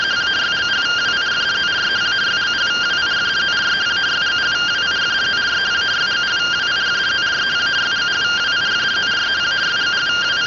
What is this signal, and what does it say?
Two-tone teleprinter signal, about 170 Hz shift.